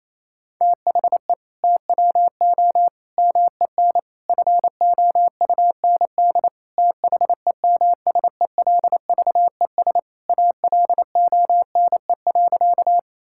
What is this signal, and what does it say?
Morse code, heard as a keyed tone, about 28 wpm, 700 Hz. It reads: THE TWO MEN FOUND THEMSELVES ALONE.
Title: Sentences from Top 1000 Words - Review Part 5 - 28wpm